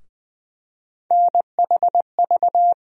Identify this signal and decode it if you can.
Morse code, heard as a keyed tone, about 20 wpm, 700 Hz. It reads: NHV
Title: USA Call Sign Postfixes - 3 Letters 20wpm